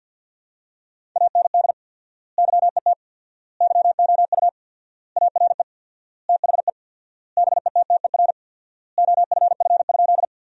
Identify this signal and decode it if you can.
Morse code, heard as a keyed tone, about 50 wpm, 700 Hz. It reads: AND YET YOU ARE THE BETTER OFF?
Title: 4x Word-Spacing - Sentences from Top 500 Words - 50wpm